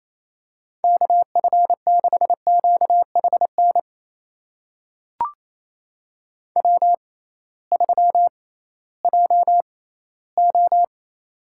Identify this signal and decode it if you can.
Morse code, heard as a keyed tone, about 28 wpm, spaced as for 16 wpm, 700 Hz. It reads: KF6QHN W3JO
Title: USA Call Signs - 6x Character Spacing - 28wpm